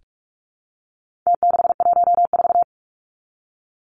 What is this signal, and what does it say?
Morse code, heard as a keyed tone, about 44 wpm, 700 Hz. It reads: T614
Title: Numbers - Contest Style - 4-Digit 45wpm